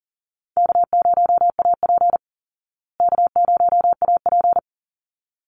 Morse code, heard as a keyed tone, about 40 wpm, 700 Hz. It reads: K0AP K0AP
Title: Easy Call Signs Repeated 2x - 40wpm